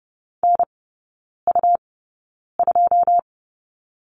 Morse code, heard as a keyed tone, about 30 wpm, spaced as for 9 wpm, 700 Hz. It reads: NU2
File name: Lesson 17e - Letter u - Three Characters 30wpm